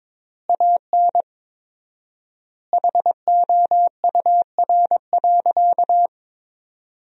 Morse code, heard as a keyed tone, about 22 wpm, 700 Hz. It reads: AN HOUR.
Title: 4x Word-Spacing - Sentences from Top 500 Words - 22wpm